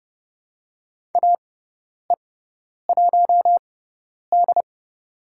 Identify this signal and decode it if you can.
Morse code, heard as a keyed tone, about 29 wpm, spaced as for 10 wpm, 700 Hz. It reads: AE1D
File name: Lesson 17g - Letter u - Call Signs 30wpm